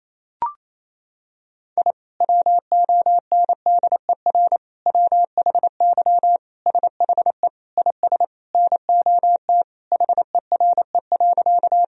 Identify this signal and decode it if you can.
Morse code, heard as a keyed tone, about 28 wpm, 700 Hz. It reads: I WONDER WHY SHE IS NOT HERE.
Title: Sentences from Top 1000 Words - Review Part 3 - 28wpm